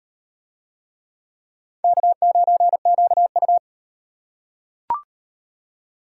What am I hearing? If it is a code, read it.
Morse code, heard as a keyed tone, about 35 wpm, 700 Hz. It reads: K9QU